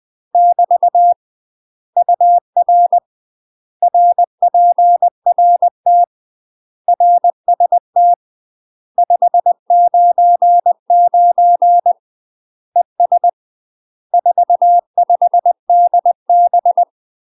Morse code, heard as a keyed tone, about 20 wpm, 700 Hz. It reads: <BT> UR RPRT RST 599 ES 45 DB
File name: UK MASTER QSOs TRAINING/20WPM